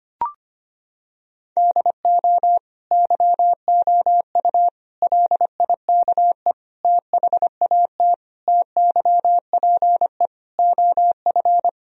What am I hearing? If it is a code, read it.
Morse code, heard as a keyed tone, about 25 wpm, 700 Hz. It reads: DO YOU LIKE THAT TYPE OF
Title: Sentences from Top 1000 Words - Review Part 3 - 25wpm